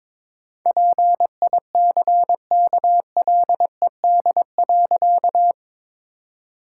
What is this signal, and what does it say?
Morse code, heard as a keyed tone, about 22 wpm, 700 Hz. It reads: PICKLED.
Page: morse-code-ninja-practice-22wpm